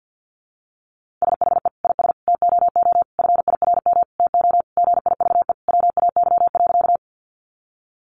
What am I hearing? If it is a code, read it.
Morse code, heard as a keyed tone, about 50 wpm, 700 Hz. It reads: SHE IS TOO FIRM TO GIVE WAY.
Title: Sentences from Top 1400 Words 50wpm